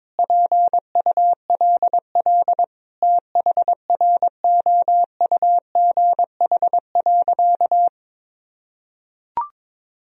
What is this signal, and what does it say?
Morse code, heard as a keyed tone, about 22 wpm, 700 Hz. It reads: PULL THROUGH.
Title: Sentences from Top 1000 Words - Review Part 2 - 22wpm